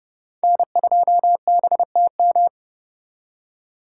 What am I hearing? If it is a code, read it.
Morse code, heard as a keyed tone, about 39 wpm, 700 Hz. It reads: N2BTM